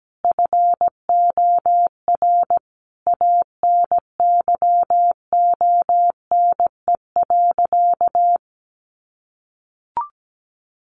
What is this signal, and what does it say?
Morse code, heard as a keyed tone, about 17 wpm, 700 Hz. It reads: FOR ANYONE.
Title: Sentences from Top 900 Words 17wpm